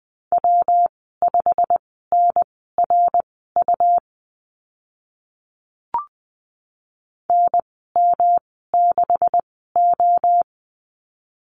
Morse code, heard as a keyed tone, about 19 wpm, spaced as for 14 wpm, 700 Hz. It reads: W5NRU NM6O
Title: 140 Lesson 23g - Letter f - Call Signs 20wpm